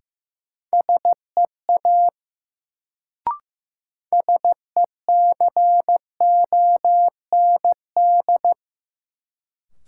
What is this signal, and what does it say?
Morse code, heard as a keyed tone, about 15 wpm, 700 Hz. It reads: SEA SECOND
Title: Sentences from Top 300 Words - New Word Drill - 15wpm